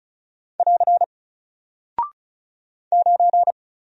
Morse code, heard as a keyed tone, about 36 wpm, 700 Hz. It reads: <AR> 9
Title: Single Character (Letter:Number:Punctuation:Prosign) Alt 35wpm